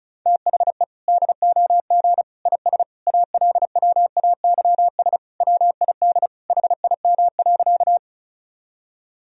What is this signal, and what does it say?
Morse code, heard as a keyed tone, about 36 wpm, 700 Hz. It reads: THE DOG IS ALWAYS WID HIM.
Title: Sentences from Top 400 Words and Abbreviations 35wpm